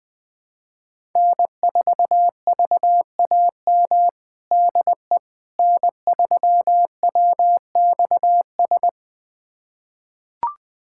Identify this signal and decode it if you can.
Morse code, heard as a keyed tone, about 21 wpm, 700 Hz. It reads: N4VAM DE N3WXS
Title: USA Callsign DE Callsign - 5 Characters - 20wpm